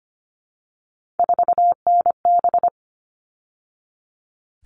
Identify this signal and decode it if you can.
Morse code, heard as a keyed tone, about 25 wpm, 700 Hz. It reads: 4NB
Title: All Cut Numbers - 3-Digit 25wpm